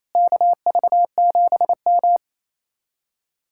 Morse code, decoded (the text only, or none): KV7M